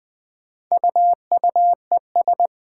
Morse code, heard as a keyed tone, about 20 wpm, 700 Hz. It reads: UUES
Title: Four Challenging Letter-Number Combinations 20wpm